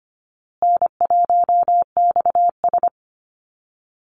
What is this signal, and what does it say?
Morse code, heard as a keyed tone, about 25 wpm, 700 Hz. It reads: N1XS
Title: Easy Call Signs Repeated 3x - 25wpm